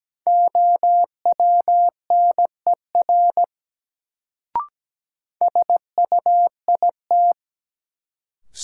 Morse code, heard as a keyed tone, about 17 wpm, 700 Hz. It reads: OWNER SUIT
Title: Sentences from Top 1600 Words - New Word Drill - 17wpm